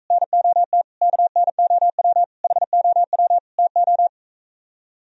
Morse code, heard as a keyed tone, about 45 wpm, 700 Hz. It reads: NOT KNOW HOW TO